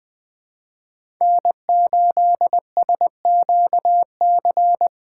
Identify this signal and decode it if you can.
Morse code, heard as a keyed tone, about 20 wpm, 700 Hz. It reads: N8SQC
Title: USA Call Signs - 2x Character Spacing - 20wpm